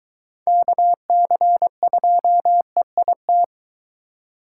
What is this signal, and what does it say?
Morse code, heard as a keyed tone, about 24 wpm, 700 Hz. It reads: KC2EIT